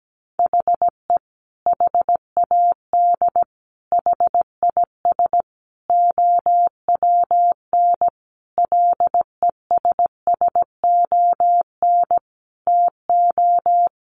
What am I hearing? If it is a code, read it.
Morse code, heard as a keyed tone, about 17 wpm, 700 Hz. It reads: HE HAD HIS OWN LESSON TO
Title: Sentences from Top 1200 Words 17wpm